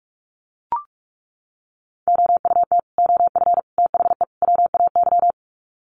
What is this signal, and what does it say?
Morse code, heard as a keyed tone, about 44 wpm, 700 Hz. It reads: OUT OF THE WAY
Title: Sets of 4 Words 45wpm